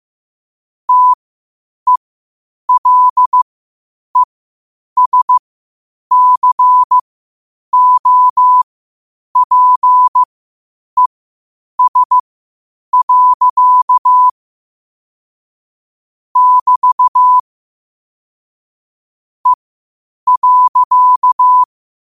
Morse code, heard as a keyed tone, about 15 wpm, spaced as for 8 wpm, 1000 Hz. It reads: TELESCOPES. <BT> E.
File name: qotdfiles / 8wpm